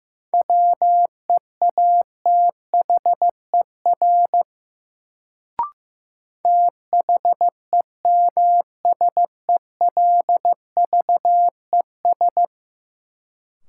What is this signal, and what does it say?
Morse code, heard as a keyed tone, about 15 wpm, 700 Hz. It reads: WEATHER THEMSELVES
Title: Sentences from Top 800 Words - New Word Drill - 15wpm